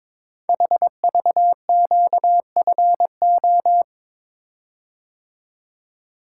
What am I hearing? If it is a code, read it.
Morse code, heard as a keyed tone, about 22 wpm, 700 Hz. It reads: HVQFO
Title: Code Groups - 5 Letters - 22wpm